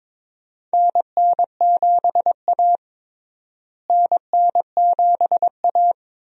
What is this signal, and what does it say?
Morse code, heard as a keyed tone, about 22 wpm, 700 Hz. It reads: NN7A NN7A
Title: USA Call Signs - 4 Characters - Repeated 2x - 22wpm